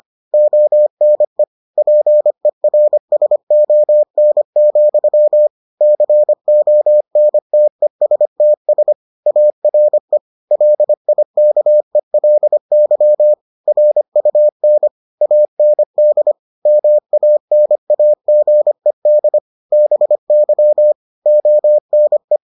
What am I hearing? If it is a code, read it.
Morse code, heard as a keyed tone, about 25 wpm, 600 Hz. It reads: ONE PERSON, CONTESTS ARE LIKELY RUN AND MANAGED BY ONE